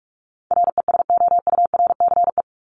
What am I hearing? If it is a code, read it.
Morse code, heard as a keyed tone, about 44 wpm, 700 Hz. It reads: RESOURCE
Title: Top 1001-1250 Words 45wpm